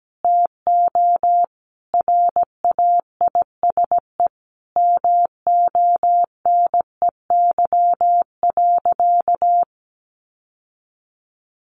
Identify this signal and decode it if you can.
Morse code, heard as a keyed tone, about 17 wpm, 700 Hz. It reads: TO RAISE MONEY.
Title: Sentences from Top 1400 Words 17wpm